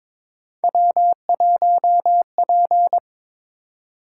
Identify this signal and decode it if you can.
Morse code, heard as a keyed tone, about 22 wpm, 700 Hz. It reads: W1P
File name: Three Letter-Number Combinations - 22wpm